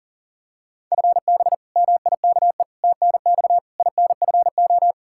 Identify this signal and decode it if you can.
Morse code, heard as a keyed tone, about 40 wpm, 700 Hz. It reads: FB MIKE TNX INFO
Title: Practice QSO - Exchange 3 - More Info - 40wpm